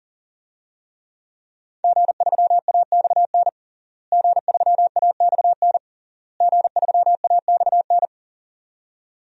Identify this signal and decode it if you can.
Morse code, heard as a keyed tone, about 40 wpm, 700 Hz. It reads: G3AXN G3AXN G3AXN
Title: DXCC-100 Call Signs - 5 Characters - Repeated 3x - 40wpm